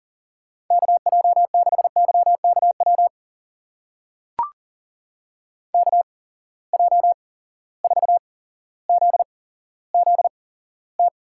Morse code, heard as a keyed tone, about 39 wpm, spaced as for 17 wpm, 700 Hz. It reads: KJ6YKW KJ4ZZT
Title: USA Call Signs - 6 Characters - 8x Character Spacing - 40wpm